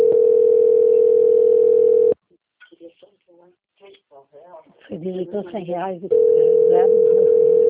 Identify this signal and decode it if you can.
Ringback tone (North America), 440 + 480 Hz, with speech.
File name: out- - - - - .30293